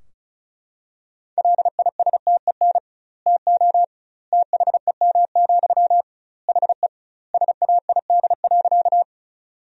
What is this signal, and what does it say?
Morse code, heard as a keyed tone, about 36 wpm, 700 Hz. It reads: LISTEN TO THEM, HE SAID.
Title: 2x Word-Spacing - Sentences from Top 400 Words - 35wpm